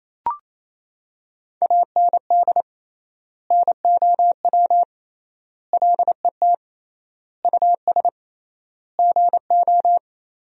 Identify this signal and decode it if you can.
Morse code, heard as a keyed tone, about 28 wpm, 700 Hz. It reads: AND NOW LET US GO
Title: 3x Word-Spacing - Sentences from Top 300 Words - 28wpm